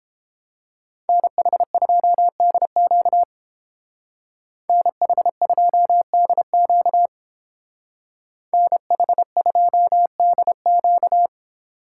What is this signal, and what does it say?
Morse code, heard as a keyed tone, about 31 wpm, 700 Hz. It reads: NH2DQ NH2DQ NH2DQ